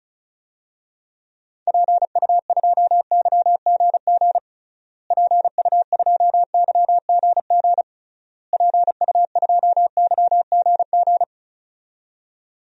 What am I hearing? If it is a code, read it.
Morse code, heard as a keyed tone, about 36 wpm, 700 Hz. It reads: PU2YGG PU2YGG PU2YGG
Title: DXCC-100 Call Signs - 6 Characters - Repeated 3x - 35wpm